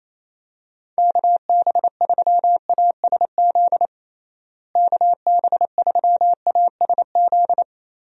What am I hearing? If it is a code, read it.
Morse code, heard as a keyed tone, about 28 wpm, 700 Hz. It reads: KB3ASZ KB3ASZ
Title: USA Call Signs - 6 Characters - Repeated 2x - 28wpm